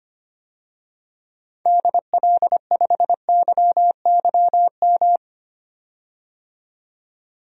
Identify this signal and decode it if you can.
Morse code, heard as a keyed tone, about 25 wpm, 700 Hz. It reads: DL5YYM
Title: DXCC-100 Call Signs - 6 Characters - 25wpm